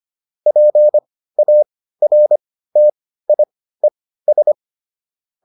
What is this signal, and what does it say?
Morse code, heard as a keyed tone, about 25 wpm, spaced as for 15 wpm, 600 Hz. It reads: PARTIES